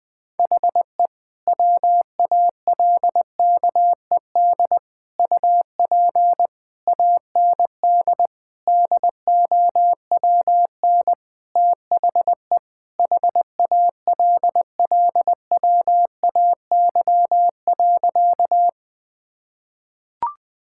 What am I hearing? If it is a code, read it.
Morse code, heard as a keyed tone, about 21 wpm, 700 Hz. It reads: HE WALKED UP AND DOWN THE HALLWAY.
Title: Sentences with 4-Word Phrases - 20wpm